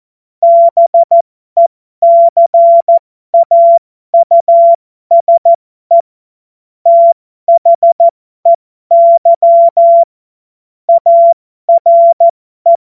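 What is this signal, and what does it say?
Morse code, heard as a keyed tone, about 14 wpm, spaced as for 12 wpm, 680 Hz. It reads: BECAUSE THEY ARE
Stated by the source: Morse joke practice at effective 60, Farnsworth 70.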